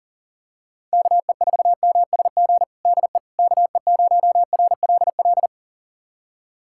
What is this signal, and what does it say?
Morse code, heard as a keyed tone, about 40 wpm, 700 Hz. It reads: KE4MSG DE KE0RLL
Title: USA Callsign DE Callsign - 6 Characters - 40wpm